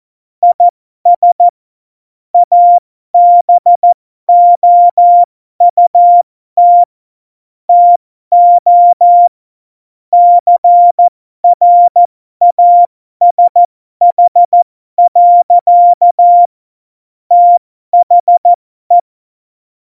Morse code, heard as a keyed tone, about 14 wpm, spaced as for 12 wpm, 700 Hz. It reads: IS ABOUT TO CRASH. THE
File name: Morsekurs / Witze